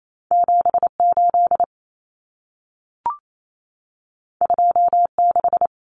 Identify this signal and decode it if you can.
Morse code, heard as a keyed tone, about 28 wpm, 700 Hz. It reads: 78 26
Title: Common Cut Numbers - 2-Digit 28wpm